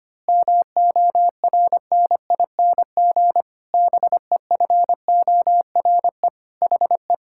Morse code, heard as a keyed tone, about 25 wpm, 700 Hz. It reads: MORNING BEFORE HE
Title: Pangram Challenge v2 - 25wpm